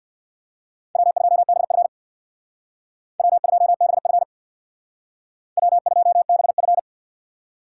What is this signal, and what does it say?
Morse code, heard as a keyed tone, about 55 wpm, 700 Hz. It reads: W2BF W2BF W2BF
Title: Speed Racing - USA 1x2 Call Signs 50wpm